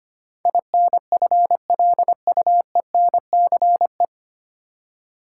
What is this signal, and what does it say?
Morse code, heard as a keyed tone, about 25 wpm, 700 Hz. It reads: INFLUENCE
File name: practice / 25wpm / 9 Letter Words - First 3 Letters Given - 25wpm